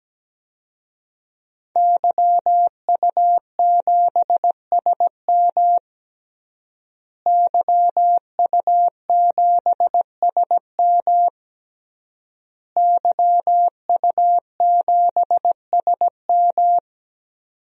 Morse code, heard as a keyed tone, about 17 wpm, 700 Hz. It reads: YU7SM YU7SM YU7SM
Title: DXCC-100 Call Signs - 5 Characters - Repeated 3x - 17wpm